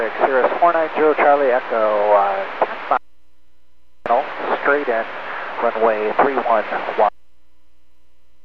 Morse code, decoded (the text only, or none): none